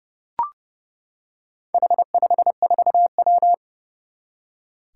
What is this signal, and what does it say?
Morse code, heard as a keyed tone, about 30 wpm, 700 Hz. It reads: H54W
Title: Four Challenging Letter-Number Combinations 30wpm